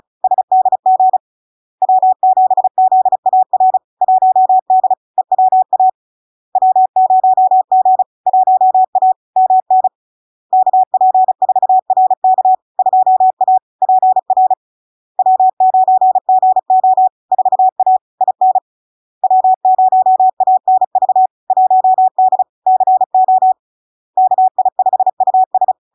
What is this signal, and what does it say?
Morse code, heard as a keyed tone, about 36 wpm, 750 Hz. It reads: SDG W7ZAR 1D EWA W0G 1A MN KP4RK 2A PR W9GO 4A IN W0ANV 1D CO KI5US
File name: LARK-FD-2022-Log-in-CW-35wpm-750-Hz